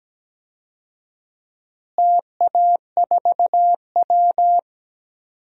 Morse code, heard as a keyed tone, about 17 wpm, 700 Hz. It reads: TA4W
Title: DXCC-100 Call Signs - 4 Characters - Repeated 3x - 17wpm